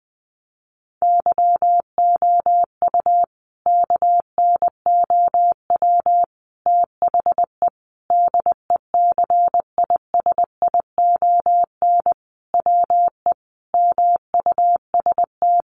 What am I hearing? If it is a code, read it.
Morse code, heard as a keyed tone, about 20 wpm, 700 Hz. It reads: YOU KNOW THE DECISION WE MUST
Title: Sentences from Top 1000 Words 20wpm